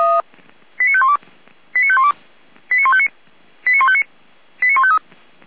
Five-tone selective call (ZVEI), heard as quick runs of five tones, tones of about 70 ms. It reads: 89512 89512 89159 89159 89153
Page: SDR/ZVEI